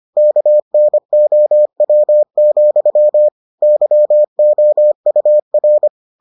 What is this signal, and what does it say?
Morse code, heard as a keyed tone, about 25 wpm, 600 Hz. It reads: KNOW, YOUR